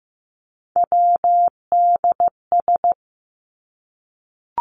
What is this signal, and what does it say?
Morse code, heard as a keyed tone, about 15 wpm, 700 Hz. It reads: WDS
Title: Abbreviations Repeated 3x - 15wpm